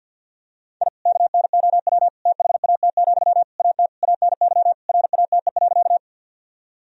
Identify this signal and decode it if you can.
Morse code, heard as a keyed tone, about 50 wpm, 700 Hz. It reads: I KNOW THAT, AT ANY RATE.
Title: Sentences from Top 1000 Words - Review Part 4 - 50wpm